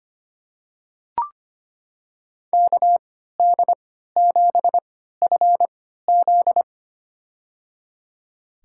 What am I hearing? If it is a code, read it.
Morse code, heard as a keyed tone, about 25 wpm, spaced as for 14 wpm, 700 Hz. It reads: KD7FZ
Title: USA Call Signs - 5 Characters - 3x Character Spacing - 25wpm